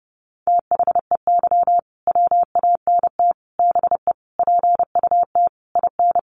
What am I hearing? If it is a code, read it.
Morse code, heard as a keyed tone, about 30 wpm, 700 Hz. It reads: THEY WANT BE PUT IN
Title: Sentences from Top 1000 Words 30wpm